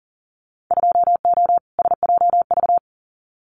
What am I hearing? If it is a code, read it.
Morse code, heard as a keyed tone, about 38 wpm, 700 Hz. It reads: 2O SJV
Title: Winter Field Day Exchanges Encore 40wpm